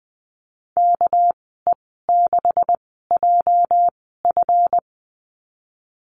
Morse code, heard as a keyed tone, about 20 wpm, spaced as for 15 wpm, 700 Hz. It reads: KE6JF
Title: 231 Lesson 37g - Letter x - Call Signs 20wpm